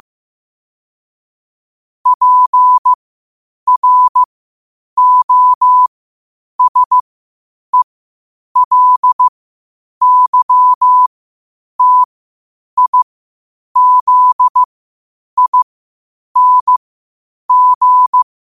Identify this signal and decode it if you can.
Morse code, heard as a keyed tone, about 15 wpm, spaced as for 8 wpm, 1000 Hz. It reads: PROSELYTIZING
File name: qotdfiles / 8wpm